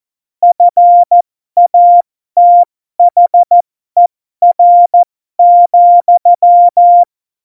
Morse code, heard as a keyed tone, about 14 wpm, spaced as for 12 wpm, 700 Hz. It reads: FATHER,
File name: Morsekurs / Witze